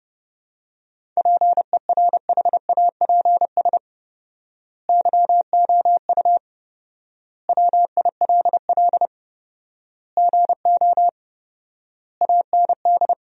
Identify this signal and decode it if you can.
Morse code, heard as a keyed tone, about 30 wpm, 700 Hz. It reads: PERHAPS YOU WILL GO AND
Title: 4x Word-Spacing - Sentences from Top 500 Words - 30wpm